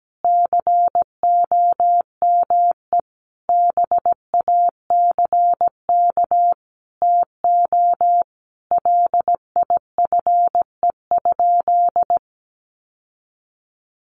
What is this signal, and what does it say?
Morse code, heard as a keyed tone, about 17 wpm, 700 Hz. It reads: COME BACK TO LIFE?
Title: Sentences from Top 1500 Words 17wpm